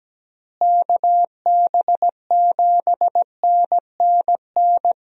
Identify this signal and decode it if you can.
Morse code, heard as a keyed tone, about 17 wpm, 700 Hz. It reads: KB7NNN
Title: USA Call Signs - 6 Characters - Repeated 2x - 17wpm